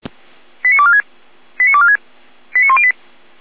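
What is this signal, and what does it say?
Five-tone selective call (ZVEI), heard as quick runs of five tones, tones of about 70 ms. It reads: 89246 89246 89108
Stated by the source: ZVEI five-tone call received on an SDR.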